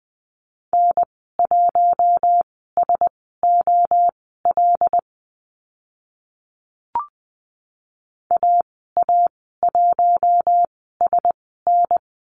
Morse code, heard as a keyed tone, about 20 wpm, spaced as for 14 wpm, 700 Hz. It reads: N1SOL AA1SN
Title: 076 Lesson 13g - Letter L - Call Signs 20wpm